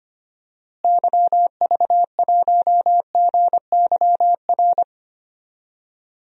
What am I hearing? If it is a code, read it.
Morse code, heard as a keyed tone, about 25 wpm, 700 Hz. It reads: YV1GYR
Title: DXCC-100 Call Signs - 6 Characters - Repeated 2x - 25wpm